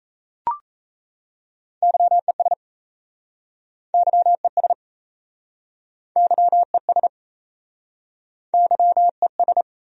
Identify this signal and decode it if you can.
Morse code, heard as a keyed tone, about 37 wpm, 700 Hz. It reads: YES YES YES YES